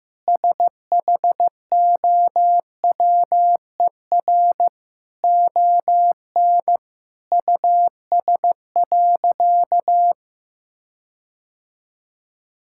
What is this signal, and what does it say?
Morse code, heard as a keyed tone, about 15 wpm, 700 Hz. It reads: SHOWER ON US.
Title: Sentences from Top 1600 Words 15wpm